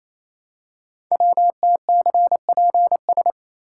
Morse code, heard as a keyed tone, about 29 wpm, 700 Hz. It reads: WTCPS
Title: Code Groups - 5 Letters - 28wpm